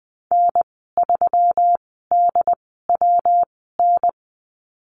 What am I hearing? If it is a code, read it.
Morse code, heard as a keyed tone, about 20 wpm, spaced as for 14 wpm, 700 Hz. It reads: N3DWN
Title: 122 Lesson 20g - Number 3 - Call Signs 20wpm